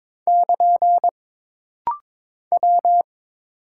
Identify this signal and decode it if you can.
Morse code, heard as a keyed tone, about 22 wpm, 700 Hz. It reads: <KN> W
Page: This rapid-fire timing is geared to push your ICR (Instant Character Recognition) skills.